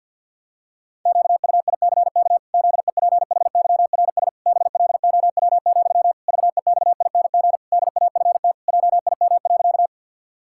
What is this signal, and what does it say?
Morse code, heard as a keyed tone, about 50 wpm, 700 Hz. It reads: QUICK ZEPHYRS BLOW, VEXING DAFT JIM.
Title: Pangram Challenge v2 - 50wpm